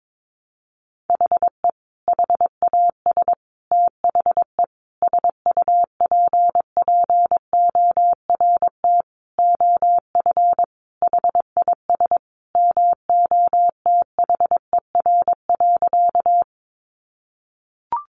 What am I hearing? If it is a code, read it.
Morse code, heard as a keyed tone, about 22 wpm, 700 Hz. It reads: HE HAS THE SUPPORT OF HIS MOTHER.
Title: Sentences from Top 800 Words 22wpm